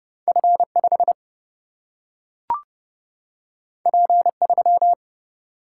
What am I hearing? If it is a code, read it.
Morse code, heard as a keyed tone, about 30 wpm, 700 Hz. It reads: F5 P3
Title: Single-Letter Single-Number 30wpm